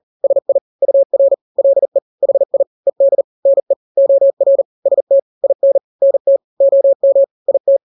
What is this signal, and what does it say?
Morse code, heard as a keyed tone, about 40 wpm, 550 Hz. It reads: SI UR LE HI ED NE OR ST IN NT OM IT
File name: n-grams / 2-grams 40 35 WPM